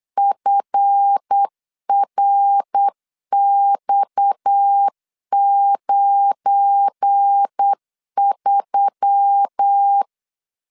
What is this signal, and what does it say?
Morse code, heard as a keyed tone, about 8 wpm, 800 Hz. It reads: FRX93
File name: LEZIONI / 7 GF9Z / 7 GF9Z 8wpm 3213